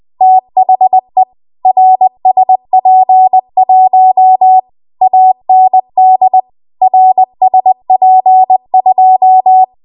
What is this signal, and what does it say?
Morse code, heard as a keyed tone, about 20 wpm, 750 Hz. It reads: THE RSP1 AND RSP2
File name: Archive / 20 WPM / 20WPM